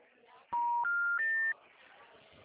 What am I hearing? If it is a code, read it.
Special information tone.